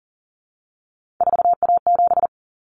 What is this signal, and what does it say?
Morse code, heard as a keyed tone, about 40 wpm, 700 Hz. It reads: 4A7
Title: Common Cut Numbers - 3-Digit 40wpm